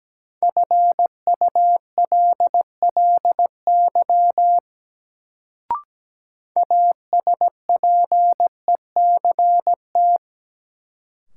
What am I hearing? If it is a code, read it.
Morse code, heard as a keyed tone, about 17 wpm, 700 Hz. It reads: FULLY ASPECT
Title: Sentences from Top 1300 Words - New Word Drill - 17wpm